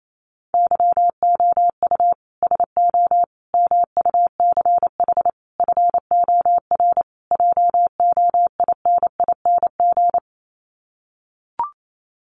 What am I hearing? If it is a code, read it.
Morse code, heard as a keyed tone, about 28 wpm, 700 Hz. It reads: YOU SO MUCH FOR JOINING
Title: Sets of 5 Words Encore2 28wpm